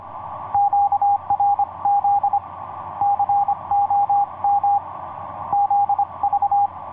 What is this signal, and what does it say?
Morse code, heard as a keyed tone, about 25 wpm, 810 Hz. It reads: QRZ COM ZV
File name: py0ff ea7jb qrt 12m cw 14feb2010 1548